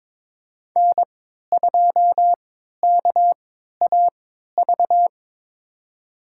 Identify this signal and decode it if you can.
Morse code, heard as a keyed tone, about 21 wpm, spaced as for 12 wpm, 700 Hz. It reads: N2KAV
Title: USA Call Signs - 5 Characters - 3x Character Spacing - 22wpm